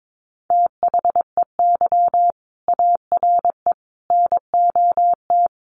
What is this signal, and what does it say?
Morse code, heard as a keyed tone, about 22 wpm, 700 Hz. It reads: THEY ARE NOT
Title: Sentences from Top 1400 Words 22wpm